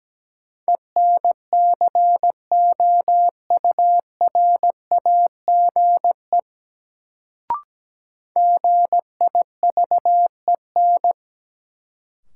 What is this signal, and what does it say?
Morse code, heard as a keyed tone, about 17 wpm, 700 Hz. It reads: ENCOURAGE GIVEN
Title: Sentences from Top 1400 Words - New Word Drill - 17wpm